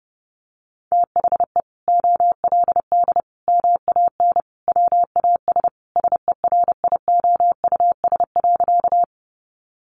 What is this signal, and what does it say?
Morse code, heard as a keyed tone, about 30 wpm, 700 Hz. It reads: THE OLD MAN WAS SERIOUS.
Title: Sentences from Top 1100 Words 30wpm